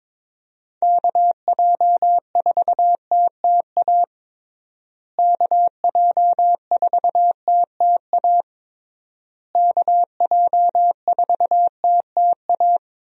Morse code, heard as a keyed tone, about 22 wpm, 700 Hz. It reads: KJ4TTA KJ4TTA KJ4TTA
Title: USA Call Signs - 6 Characters - Repeated 3x - 22wpm